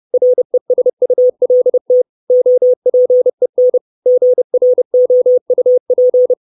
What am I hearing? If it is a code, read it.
Morse code, heard as a keyed tone, about 30 wpm, 500 Hz. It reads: RESULT OPEN GROUP
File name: 632 Words 30WPM